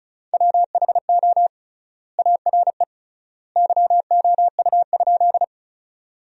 Morse code, heard as a keyed tone, about 34 wpm, 700 Hz. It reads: WHO ARE YOU?
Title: 3x Word-Spacing - Sentences from Top 100 Words - 35wpm